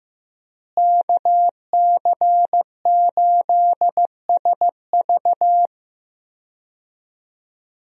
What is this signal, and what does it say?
Morse code, heard as a keyed tone, about 16 wpm, 700 Hz. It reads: KC8SV
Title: USA Call Signs - 5 Characters - Repeated 3x - 15wpm